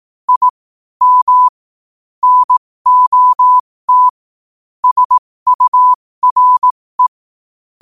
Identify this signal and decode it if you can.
Morse code, heard as a keyed tone, about 18 wpm, spaced as for 15 wpm, 1000 Hz. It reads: I M NOT SURE
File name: qotdfiles / 15wpm